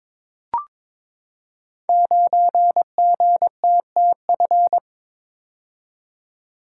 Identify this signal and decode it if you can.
Morse code, heard as a keyed tone, about 22 wpm, 700 Hz. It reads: 9GTTF
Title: Code Groups - 5 Letter:Numbers - 22wpm